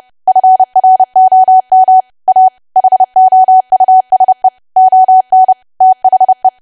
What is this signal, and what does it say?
Morse code, heard as a keyed tone, about 30 wpm, 750 Hz. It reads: FROM A HOUSE ON THE
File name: cwa / W1AW-Level3 / 30WPM